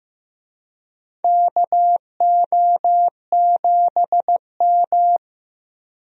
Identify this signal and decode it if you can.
Morse code, heard as a keyed tone, about 15 wpm, 700 Hz. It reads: KO7M
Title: USA Call Signs - 4 Characters - 15wpm